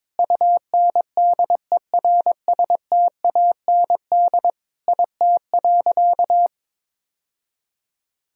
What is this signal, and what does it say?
Morse code, heard as a keyed tone, about 22 wpm, 700 Hz. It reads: UNDERSTAND IT.
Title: Sentences from Top 1000 Words - Review Part 2 - 22wpm